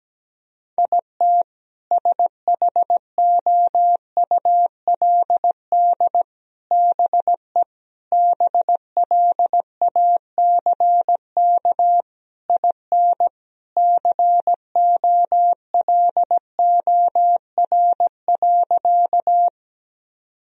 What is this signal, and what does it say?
Morse code, heard as a keyed tone, about 17 wpm, 700 Hz. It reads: IT SHOULD BE BLACK IN COLOR.
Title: Sentences from Top 1000 Words - Review Part 2 - 17wpm